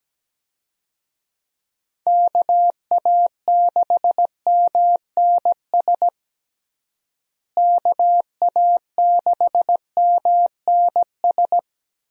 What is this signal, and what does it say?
Morse code, heard as a keyed tone, about 17 wpm, 700 Hz. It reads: KA6MNS KA6MNS
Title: USA Call Signs - 6 Characters - Repeated 3x - 17wpm